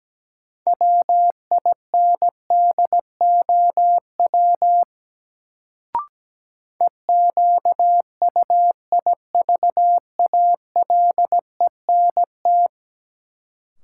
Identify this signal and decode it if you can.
Morse code, heard as a keyed tone, about 17 wpm, 700 Hz. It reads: WINDOW EQUIVALENT
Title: Sentences from Top 1600 Words - New Word Drill Alt - 17wpm